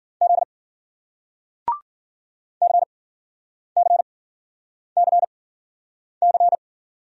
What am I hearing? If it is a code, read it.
Morse code, heard as a keyed tone, about 54 wpm, spaced as for 9 wpm, 700 Hz. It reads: Z CCCC